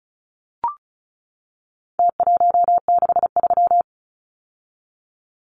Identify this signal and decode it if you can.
Morse code, heard as a keyed tone, about 34 wpm, 700 Hz. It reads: T163
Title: Numbers - Contest Style - 4-Digit 35wpm